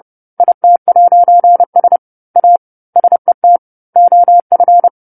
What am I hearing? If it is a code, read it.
Morse code, heard as a keyed tone, about 30 wpm, 700 Hz. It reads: IT'S A SET OF
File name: 10-Chapter-10 30 WPM